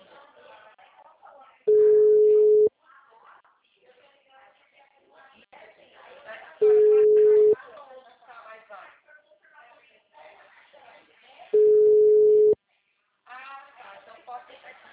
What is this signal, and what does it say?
Ringback tone (Europe, 425 Hz), with speech.